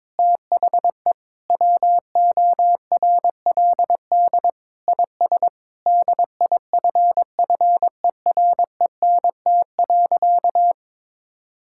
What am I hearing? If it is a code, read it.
Morse code, heard as a keyed tone, about 22 wpm, 700 Hz. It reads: THE WORLD IS DIFFERENT.
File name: practice / 22wpm / Sentences with 4-Word Phrases - 22wpm